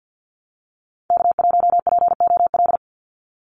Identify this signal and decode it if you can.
Morse code, heard as a keyed tone, about 51 wpm, 700 Hz. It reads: K1POL
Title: USA Call Signs 50wpm